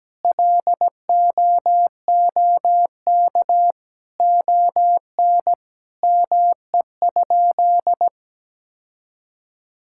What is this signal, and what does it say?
Morse code, heard as a keyed tone, about 17 wpm, 700 Hz. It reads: LOOK ON ME?
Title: Sentences from Top 1000 Words - Review Part 5 - 17wpm